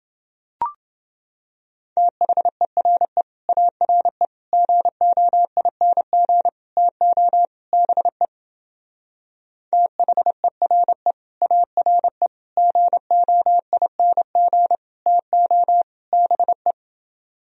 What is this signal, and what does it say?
Morse code, heard as a keyed tone, about 28 wpm, 700 Hz. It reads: THERE ARE GOING TO BE THERE ARE GOING TO BE